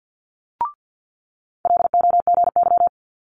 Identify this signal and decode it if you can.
Morse code, heard as a keyed tone, about 50 wpm, 700 Hz. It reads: LOGY
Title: Multi-letter Phonemes Quick Fire 50wpm